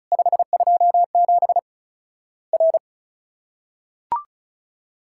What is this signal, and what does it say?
Morse code, heard as a keyed tone, about 36 wpm, 700 Hz. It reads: 527 R
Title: Worked All Europe DX Contest - QTC Exchange - 35wpm